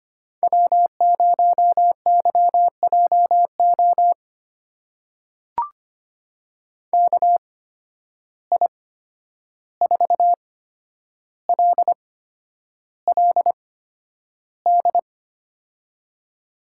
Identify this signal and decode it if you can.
Morse code, heard as a keyed tone, about 25 wpm, spaced as for 10 wpm, 700 Hz. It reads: W0YJO KI4LLD
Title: USA Call Signs - 8x Character Spacing - 25wpm